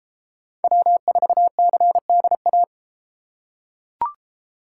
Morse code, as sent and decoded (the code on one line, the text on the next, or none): .-- ....- -.-. -.. .-
W4CDA